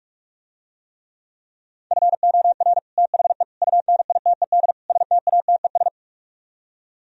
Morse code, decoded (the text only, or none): FOR THE UNITED STATES